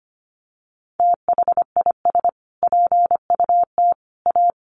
Morse code, heard as a keyed tone, about 25 wpm, 700 Hz. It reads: THIS PUT A
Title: Sentences from Top 1300 Words 25wpm